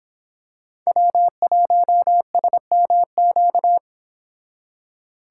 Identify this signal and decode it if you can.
Morse code, heard as a keyed tone, about 27 wpm, 700 Hz. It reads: W1SMQ
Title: Speed Racing - USA 1x3 Call Signs 22wpm